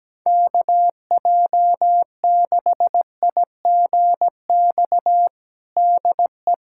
Morse code, heard as a keyed tone, about 17 wpm, 700 Hz. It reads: KJ6IGX DE
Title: USA Callsign DE Callsign - 6 Characters - 17wpm